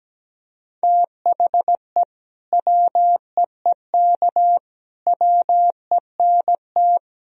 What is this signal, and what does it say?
Morse code, heard as a keyed tone, about 17 wpm, 700 Hz. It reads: THE WEEK WENT
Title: Sentences from Top 1000 Words - Review Part 2 - 17wpm